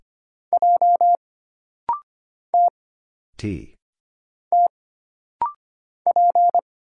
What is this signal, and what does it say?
Morse code, heard as a keyed tone, about 25 wpm, 700 Hz. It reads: J T T P
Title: Single Letter-Number Alt - Rapid-Fire 25wpm